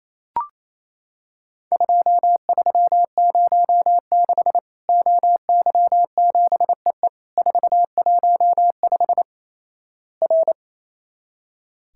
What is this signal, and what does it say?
Morse code, heard as a keyed tone, about 28 wpm, 700 Hz. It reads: 2306 OY7EE 415 R
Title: Worked All Europe DX Contest - QTC Exchange - 28wpm